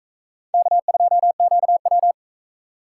Morse code, heard as a keyed tone, about 43 wpm, 700 Hz. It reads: K2QW